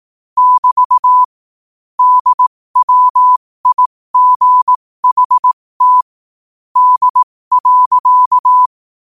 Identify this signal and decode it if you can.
Morse code, heard as a keyed tone, about 18 wpm, spaced as for 15 wpm, 1000 Hz. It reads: <BT> DWIGHT D.
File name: qotdfiles / 15wpm